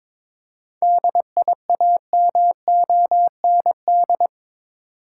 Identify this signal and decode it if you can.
Morse code, heard as a keyed tone, about 23 wpm, 700 Hz. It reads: DIAMOND
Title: Seven Letter Words - 22wpm